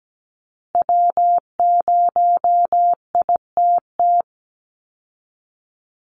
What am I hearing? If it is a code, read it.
Morse code, heard as a keyed tone, about 17 wpm, 700 Hz. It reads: W0ITT
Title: Call Signs Repeated 2x - 17wpm